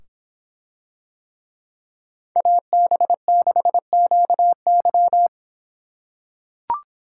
Morse code, heard as a keyed tone, about 26 wpm, 700 Hz. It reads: AB6QY